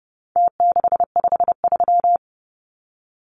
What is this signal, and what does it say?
Morse code, heard as a keyed tone, about 30 wpm, 700 Hz. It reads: T653
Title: Numbers - Contest Style - 4-Digit 30wpm